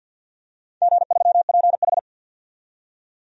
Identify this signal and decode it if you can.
Morse code, heard as a keyed tone, about 46 wpm, 700 Hz. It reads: G3PH
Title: DXCC-100 Call Signs - 4 Characters - 50wpm